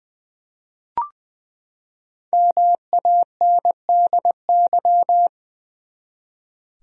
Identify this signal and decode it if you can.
Morse code, heard as a keyed tone, about 20 wpm, 700 Hz. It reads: MANDY
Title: USA Common Names - Max 5 Letters - 20wpm